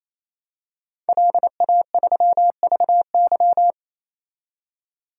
Morse code, heard as a keyed tone, about 28 wpm, 700 Hz. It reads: LA3VY
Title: DXCC-100 Call Signs - 5 Characters - 28wpm